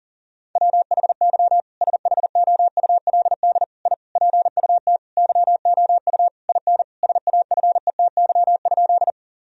Morse code, heard as a keyed tone, about 40 wpm, 700 Hz. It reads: WHY SHOULD I PUT YOU IN SAFETY?
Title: Sentences from Top 1000 Words - Review Part 4 - 40wpm